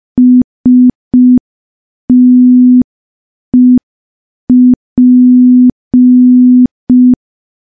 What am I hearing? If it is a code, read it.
Morse code, heard as a keyed tone, about 5 wpm, 250 Hz. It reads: STEP
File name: morse / morse-training-05WPM